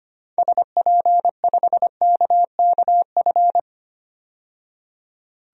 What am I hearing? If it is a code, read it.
Morse code, heard as a keyed tone, about 25 wpm, 700 Hz. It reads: SP5KKF
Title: DXCC-100 Call Signs - 6 Characters - 25wpm